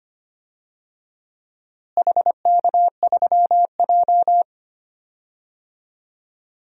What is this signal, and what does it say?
Morse code, heard as a keyed tone, about 25 wpm, 700 Hz. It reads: HK3J
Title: DXCC-100 Call Signs - 4 Characters - Repeated 3x - 25wpm